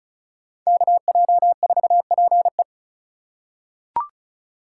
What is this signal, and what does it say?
Morse code, heard as a keyed tone, about 36 wpm, 700 Hz. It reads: KJ4PE
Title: USA Call Signs - 4x Character Spacing - 35wpm